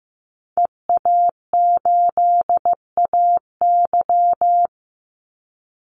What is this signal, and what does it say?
Morse code, heard as a keyed tone, about 15 wpm, 700 Hz. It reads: EA8AY
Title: Call Signs Repeated 3x - 15wpm